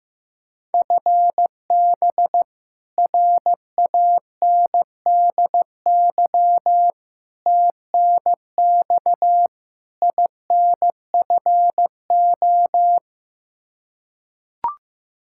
Morse code, heard as a keyed tone, about 15 wpm, 700 Hz. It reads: FB RANDY TNX INFO
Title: Practice QSO - Exchange 3 - More Info - 15wpm